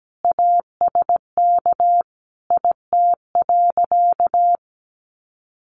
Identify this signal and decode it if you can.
Morse code, heard as a keyed tone, about 17 wpm, 700 Hz. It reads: ASK IT.
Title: Sentences from Top 1300 Words 17wpm